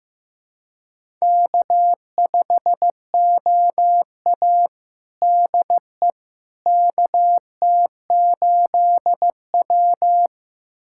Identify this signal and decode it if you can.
Morse code, heard as a keyed tone, about 16 wpm, 700 Hz. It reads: K5OA DE KT8W
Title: USA Callsign DE Callsign - 4 Characters - 15wpm